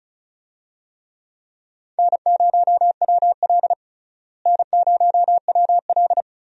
Morse code, heard as a keyed tone, about 36 wpm, 700 Hz. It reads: N0WL N0WL
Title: USA Call Signs - 4 Characters - Repeated 2x - 35wpm